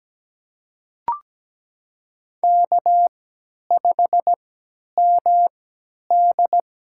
Morse code, heard as a keyed tone, about 17 wpm, spaced as for 10 wpm, 700 Hz. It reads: K5MD
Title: USA Call Signs - 4 Characters - 3x Character Spacing - 17wpm